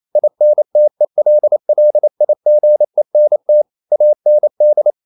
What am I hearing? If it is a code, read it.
Morse code, heard as a keyed tone, about 28 wpm, 600 Hz. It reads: INTELLIGENT AND
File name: Never 0027wpm